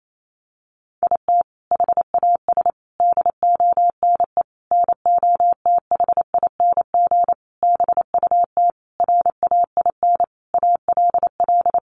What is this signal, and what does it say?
Morse code, heard as a keyed tone, about 28 wpm, 700 Hz. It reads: IT HAS DONE NOTHING BUT RAIN ALL
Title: Sentences from Top 1100 Words 28wpm